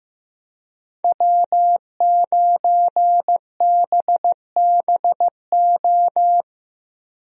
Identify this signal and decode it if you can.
Morse code, heard as a keyed tone, about 15 wpm, 700 Hz. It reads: W9BBO
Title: USA Call Signs - 5 Characters - 15wpm